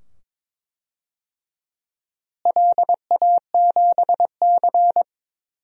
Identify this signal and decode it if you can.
Morse code, heard as a keyed tone, about 22 wpm, 700 Hz. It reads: LA7C
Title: DXCC-100 Call Signs - 4 Characters - Repeated 2x - 22wpm